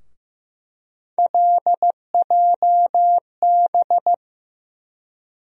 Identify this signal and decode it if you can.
Morse code, heard as a keyed tone, about 15 wpm, 700 Hz. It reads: LJB
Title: USA Call Sign Postfixes - 3 Letters 15wpm